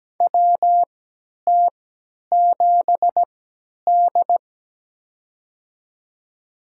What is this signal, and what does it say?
Morse code, heard as a keyed tone, about 17 wpm, spaced as for 10 wpm, 700 Hz. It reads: WT7D
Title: USA Call Signs - 3x Character Spacing - 17wpm